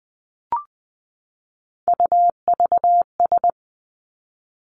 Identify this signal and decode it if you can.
Morse code, heard as a keyed tone, about 20 wpm, 700 Hz. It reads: UVS
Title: Three Challenging Letter-Number Combinations 20wpm